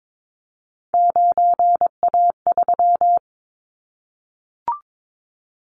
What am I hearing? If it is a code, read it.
Morse code, heard as a keyed tone, about 22 wpm, 700 Hz. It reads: 9A3
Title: Callsign Prefixes for Top 50 DXCC - 22wpm